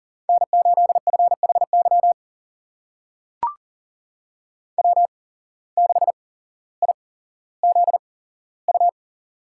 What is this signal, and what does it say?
Morse code, heard as a keyed tone, about 37 wpm, spaced as for 17 wpm, 700 Hz. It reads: N8FHY W6IZU
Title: USA Call Signs - 5 Characters - 8x Character Spacing - 40wpm